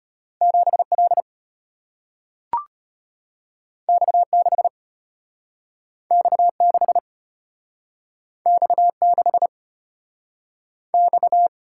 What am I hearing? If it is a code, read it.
Morse code, heard as a keyed tone, about 34 wpm, 700 Hz. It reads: 7L X6 X6 X6 X